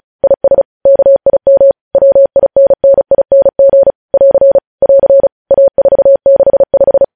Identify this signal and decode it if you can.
Morse code, heard as a keyed tone, about 36 wpm, 550 Hz. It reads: IS KIM WINNING <AR> <AR> A465
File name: podcast / News headlines-1x-35x35